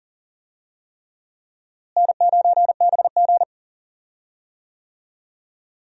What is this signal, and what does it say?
Morse code, heard as a keyed tone, about 40 wpm, 700 Hz. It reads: N9BG
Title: USA Call Signs - 4 Characters - Repeated 3x - 40wpm